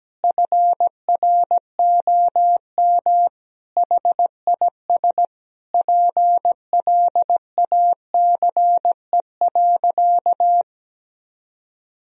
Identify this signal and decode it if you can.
Morse code, heard as a keyed tone, about 17 wpm, 700 Hz. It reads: FROM HIS PLACE.
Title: Sentences from Top 1000 Words - Review Part 1 - 17wpm